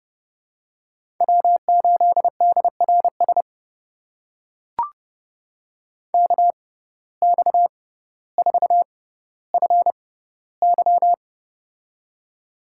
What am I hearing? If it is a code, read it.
Morse code, heard as a keyed tone, about 31 wpm, spaced as for 16 wpm, 700 Hz. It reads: W8DRS KX4FY
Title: USA Call Signs - 5 Characters - 6x Character Spacing - 30wpm